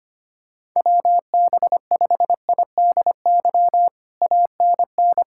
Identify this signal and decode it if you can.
Morse code, heard as a keyed tone, about 25 wpm, 700 Hz. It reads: WB5IDY ANN
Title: K1USN SST QSO Practice - 25wpm